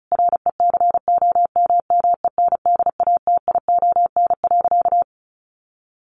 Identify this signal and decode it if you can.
Morse code, heard as a keyed tone, about 34 wpm, 700 Hz. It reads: RECOMMENDATION.
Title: Sentences from Top 1400 Words 35wpm